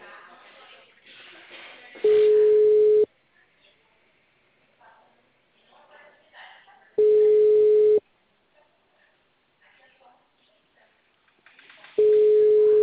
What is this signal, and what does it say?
Ringback tone (Europe, 425 Hz), with speech.